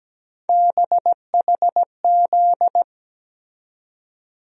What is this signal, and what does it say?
Morse code, heard as a keyed tone, about 17 wpm, 700 Hz. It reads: BHZ
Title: USA Call Sign Postfixes - 3 Letters 17wpm